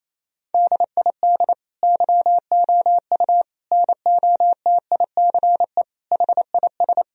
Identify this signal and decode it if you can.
Morse code, heard as a keyed tone, about 28 wpm, 700 Hz. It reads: DID YOU NOTICE HIS
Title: Sentences from Top 1000 Words - Review Part 3 - 28wpm